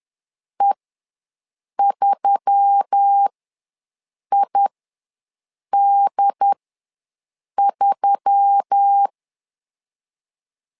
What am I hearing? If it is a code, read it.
Morse code, heard as a keyed tone, about 10 wpm, spaced as for 6 wpm, 800 Hz. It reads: E3ID3
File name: LEZIONI / 4 Q1S3 ws 10wpm 2437